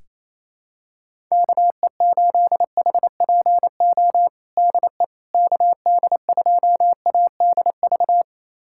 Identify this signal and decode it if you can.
Morse code, heard as a keyed tone, about 28 wpm, 700 Hz. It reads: KE8HPO DE KD2ADV
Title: USA Callsign DE Callsign - 6 Characters - 28wpm